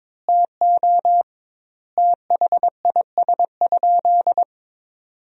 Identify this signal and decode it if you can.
Morse code, heard as a keyed tone, about 21 wpm, 700 Hz. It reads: TO THIS?
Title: 2x Word-Spacing - Sentences from Top 100 Words - 22wpm